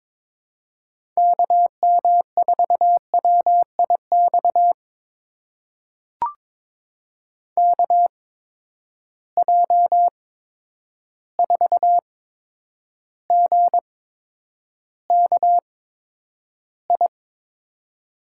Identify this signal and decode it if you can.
Morse code, heard as a keyed tone, about 22 wpm, spaced as for 9 wpm, 700 Hz. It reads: KM4WIX KJ4GKI
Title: USA Call Signs - 6 Characters - 8x Character Spacing - 22wpm